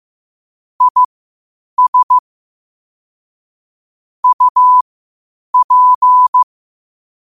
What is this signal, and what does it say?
Morse code, heard as a keyed tone, about 15 wpm, spaced as for 8 wpm, 1000 Hz. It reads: IS UP
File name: qotdfiles / 8wpm